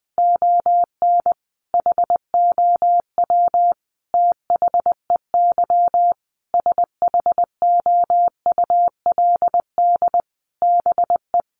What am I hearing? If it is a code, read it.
Morse code, heard as a keyed tone, about 20 wpm, 700 Hz. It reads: ON HOW THEY SHOULD BE
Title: Sentences from Top 1300 Words 20wpm